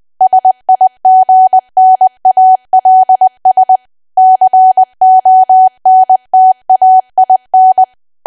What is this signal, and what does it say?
Morse code, heard as a keyed tone, about 20 wpm, 750 Hz. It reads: SIGNALS CONTAIN
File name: cwa / W1AW-Level3 / 20WPM